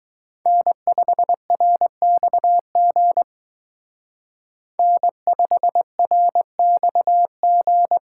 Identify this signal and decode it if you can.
Morse code, heard as a keyed tone, about 21 wpm, 700 Hz. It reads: N5RXG N5RXG
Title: Speed Racing - USA 1x3 Call Signs 17wpm